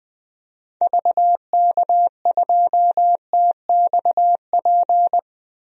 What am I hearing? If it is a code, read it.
Morse code, heard as a keyed tone, about 20 wpm, 700 Hz. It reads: VK2TXP
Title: DXCC-100 Call Signs - 6 Characters - 20wpm